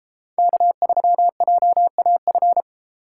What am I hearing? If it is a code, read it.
Morse code, heard as a keyed tone, about 31 wpm, 700 Hz. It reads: K3JAF